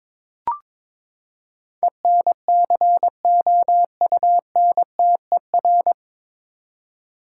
Morse code, heard as a keyed tone, about 23 wpm, 700 Hz. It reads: ENCOUNTER
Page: Morse Code Ninja practice 22wpm